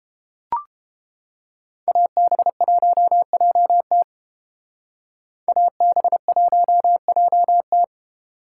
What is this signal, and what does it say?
Morse code, heard as a keyed tone, about 33 wpm, 700 Hz. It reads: AB1JT AB1JT